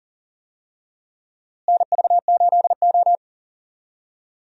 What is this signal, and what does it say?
Morse code, heard as a keyed tone, about 40 wpm, 700 Hz. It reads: NV8O
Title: USA Call Signs - 4 Characters - Repeated 3x - 40wpm